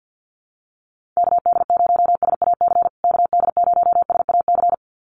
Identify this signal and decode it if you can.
Morse code, heard as a keyed tone, about 48 wpm, 700 Hz. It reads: KD0SAC KD0SAC
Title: USA Call Signs Repeated 2x 50wpm